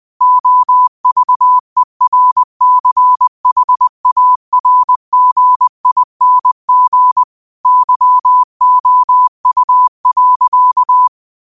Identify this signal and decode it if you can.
Morse code, heard as a keyed tone, about 20 wpm, 1000 Hz. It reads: OVERCHARGING YOU.
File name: qotdfiles / 20wpm